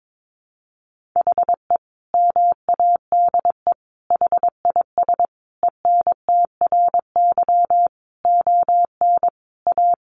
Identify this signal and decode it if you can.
Morse code, heard as a keyed tone, about 22 wpm, 700 Hz. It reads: HE MADE HIS ENTRY ON A
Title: Sentences from Top 1500 Words 22wpm